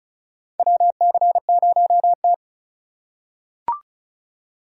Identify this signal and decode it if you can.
Morse code, heard as a keyed tone, about 36 wpm, 700 Hz. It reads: WC0T
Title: USA Call Signs - 4 Characters - 2x Character Spacing - 35wpm